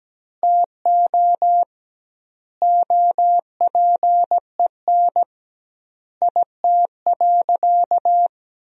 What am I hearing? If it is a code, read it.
Morse code, heard as a keyed tone, about 17 wpm, 700 Hz. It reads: TO OPEN IT.
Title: 2x Word-Spacing - Sentences from Top 300 Words - 17wpm